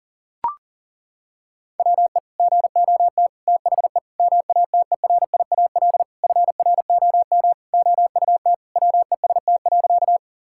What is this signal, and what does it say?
Morse code, heard as a keyed tone, about 40 wpm, 700 Hz. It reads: WE GOT THE MATERIAL FROM OUT WEST.
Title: Sentences from Top 1000 Words - Review Part 5 - 40wpm